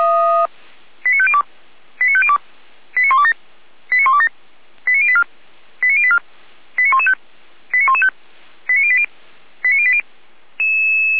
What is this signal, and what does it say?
Five-tone selective call (ZVEI), heard as quick runs of five tones, tones of about 70 ms. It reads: 89502 89502 89127 89127 89084 89084 89115 89115 89080 89080